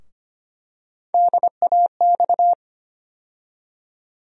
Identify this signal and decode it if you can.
Morse code, heard as a keyed tone, about 25 wpm, 700 Hz. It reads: DAX
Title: USA Call Sign Postfixes - 3 Letters 25wpm